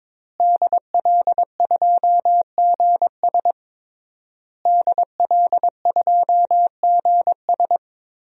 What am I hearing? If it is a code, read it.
Morse code, heard as a keyed tone, about 22 wpm, 700 Hz. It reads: DL2GS DL2GS
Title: DXCC-100 Call Signs - 5 Characters - Repeated 3x - 22wpm